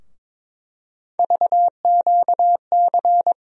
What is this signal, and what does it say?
Morse code, heard as a keyed tone, about 22 wpm, 700 Hz. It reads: VQC
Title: USA Call Sign Postfixes - 3 Letters 22wpm